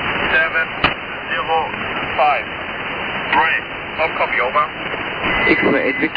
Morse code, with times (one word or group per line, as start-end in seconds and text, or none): none